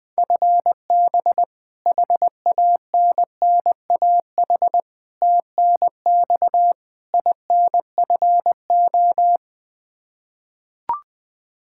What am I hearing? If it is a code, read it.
Morse code, heard as a keyed tone, about 20 wpm, 700 Hz. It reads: FB HANNAH TNX INFO
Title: Practice QSO - Exchange 3 - More Info - 20wpm